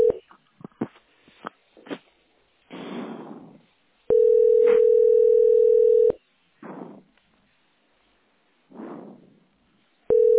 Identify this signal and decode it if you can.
Ringback tone (North America), 440 + 480 Hz.